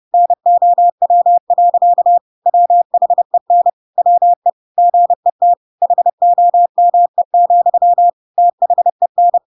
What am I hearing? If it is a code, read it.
Morse code, heard as a keyed tone, about 30 wpm, 700 Hz. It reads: NOW. WHEN WE GET HOME, THEN
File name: Pinocchio / Pinocchio ch3 30 30 ik1hge